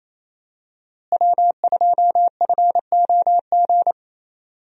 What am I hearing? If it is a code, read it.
Morse code, heard as a keyed tone, about 27 wpm, 700 Hz. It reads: W2FOG
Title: USA Call Signs - 5 Characters - Repeated 3x - 28wpm